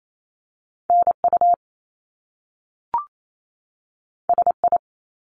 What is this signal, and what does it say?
Morse code, heard as a keyed tone, about 28 wpm, 700 Hz. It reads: NU SI
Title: Top Two Letter Combinations in Words 28wpm